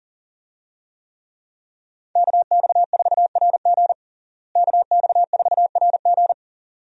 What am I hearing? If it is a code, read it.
Morse code, heard as a keyed tone, about 40 wpm, 700 Hz. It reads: KX4RG KX4RG
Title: USA Call Signs - 5 Characters - Repeated 2x - 40wpm